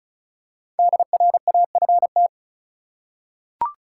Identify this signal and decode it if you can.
Morse code, heard as a keyed tone, about 34 wpm, 700 Hz. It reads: DRAFT
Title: Five Letter Words - 35wpm